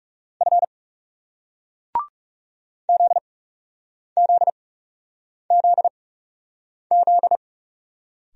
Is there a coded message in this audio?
Yes: Morse code from 0.4 to 7.4 s, heard as a keyed tone, about 38 wpm, spaced as for 8 wpm, 700 Hz. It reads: F ZZZZ